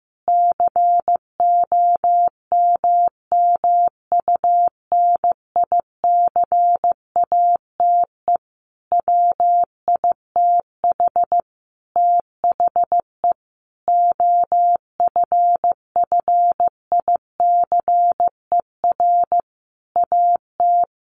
Morse code, heard as a keyed tone, about 15 wpm, 700 Hz. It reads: COMMUNICATE WITH THE OFFICER AT
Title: Sentences from Top 1300 Words 15wpm